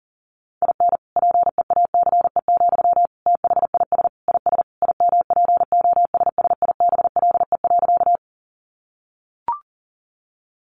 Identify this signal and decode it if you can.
Morse code, heard as a keyed tone, about 40 wpm, 700 Hz. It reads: IN PEACE, THIS IS IMPOSSIBLE.
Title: Sentences from Top 1500 Words 40wpm